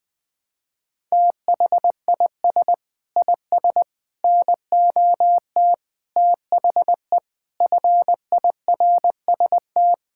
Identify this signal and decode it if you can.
Morse code, heard as a keyed tone, about 20 wpm, 700 Hz. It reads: THIS IS NOT THE FIRST
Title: Sets of 6 Words - 20wpm